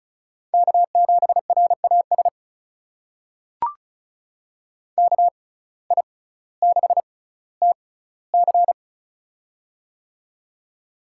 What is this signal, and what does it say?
Morse code, heard as a keyed tone, about 34 wpm, spaced as for 18 wpm, 700 Hz. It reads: K7RAS KI6TC
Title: USA Call Signs - 5 Characters - 6x Character Spacing - 35wpm